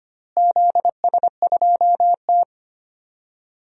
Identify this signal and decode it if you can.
Morse code, heard as a keyed tone, about 25 wpm, 700 Hz. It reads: ZS2T
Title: DXCC-100 Call Signs - 4 Characters - Repeated 3x - 25wpm